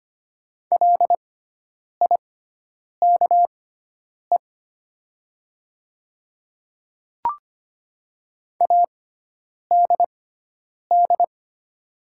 Morse code, heard as a keyed tone, about 25 wpm, spaced as for 9 wpm, 700 Hz. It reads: LIKE ADD
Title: 6x Character Spacing - Top 200 Words - 25wpm